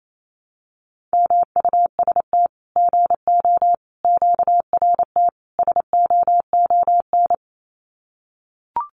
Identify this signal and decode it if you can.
Morse code, heard as a keyed tone, about 28 wpm, 700 Hz. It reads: MUST GO QRT SOON
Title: Common QSO Phrases 28wpm